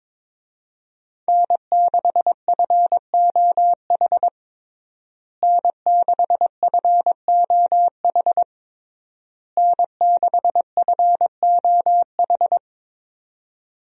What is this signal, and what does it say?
Morse code, heard as a keyed tone, about 22 wpm, 700 Hz. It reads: N6FOH N6FOH N6FOH
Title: USA Call Signs - 5 Characters - Repeated 3x - 22wpm